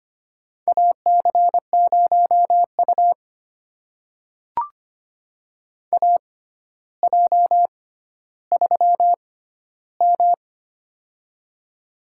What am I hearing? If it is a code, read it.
Morse code, heard as a keyed tone, about 25 wpm, spaced as for 13 wpm, 700 Hz. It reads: AC0U AJ3M
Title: USA Call Signs - 4 Characters - 6x Character Spacing - 25wpm